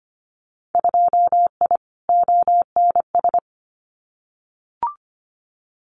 Morse code, heard as a keyed tone, about 25 wpm, 690 Hz. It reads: 2I ONS
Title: Winter Field Day Exchanges Encore 25wpm